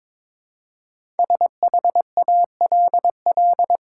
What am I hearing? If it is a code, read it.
Morse code, heard as a keyed tone, about 22 wpm, 700 Hz. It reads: SHALL
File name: 5 Letter Words - First 3 Letters Given - 22wpm